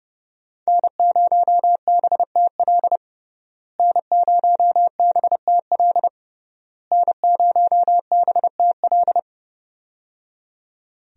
Morse code, heard as a keyed tone, about 31 wpm, 700 Hz. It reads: N0BTL N0BTL N0BTL
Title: USA Call Signs - 5 Characters - Repeated 3x - 30wpm